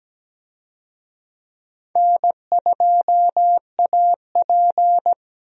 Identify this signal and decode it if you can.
Morse code, heard as a keyed tone, about 17 wpm, 700 Hz. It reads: N2AP
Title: USA Call Signs - 4 Characters - Repeated 3x - 17wpm